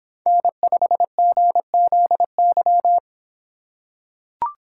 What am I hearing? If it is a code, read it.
Morse code, heard as a keyed tone, about 26 wpm, 700 Hz. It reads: N5GZY